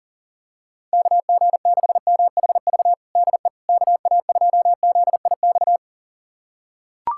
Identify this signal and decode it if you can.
Morse code, heard as a keyed tone, about 37 wpm, 700 Hz. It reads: KG6MHV DE KA2ZIX
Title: USA Callsign DE Callsign 40wpm